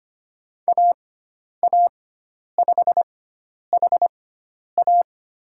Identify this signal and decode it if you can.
Morse code, heard as a keyed tone, about 25 wpm, spaced as for 10 wpm, 700 Hz. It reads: AA5HA
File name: Lesson 16g - Letter c - Call Signs 25wpm